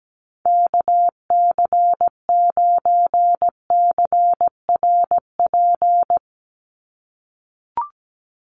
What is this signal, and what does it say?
Morse code, heard as a keyed tone, about 17 wpm, 700 Hz. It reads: KC9CRP
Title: USA Call Signs 17wpm